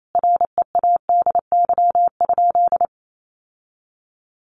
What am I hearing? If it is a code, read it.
Morse code, heard as a keyed tone, about 28 wpm, 700 Hz. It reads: READY?
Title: Sentences from Top 1400 Words 28wpm